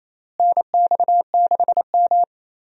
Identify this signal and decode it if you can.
Morse code, heard as a keyed tone, about 28 wpm, 700 Hz. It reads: NX6M
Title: USA Call Signs - 4 Characters - Repeated 2x - 28wpm